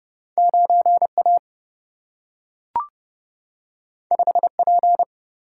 Morse code, heard as a keyed tone, about 30 wpm, 700 Hz. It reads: 9 A 5 P
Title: Single-Letter Single-Number 30wpm